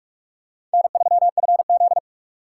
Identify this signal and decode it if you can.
Morse code, heard as a keyed tone, about 44 wpm, 700 Hz. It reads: N3FZ